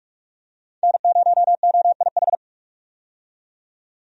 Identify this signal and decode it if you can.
Morse code, heard as a keyed tone, about 46 wpm, 700 Hz. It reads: N0OIH
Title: USA Call Signs - 5 Characters - 45wpm